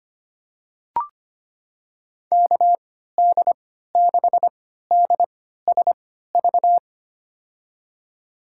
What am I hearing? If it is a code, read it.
Morse code, heard as a keyed tone, about 25 wpm, spaced as for 14 wpm, 700 Hz. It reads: KD6DSV
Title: USA Call Signs - 6 Characters - 3x Character Spacing - 25wpm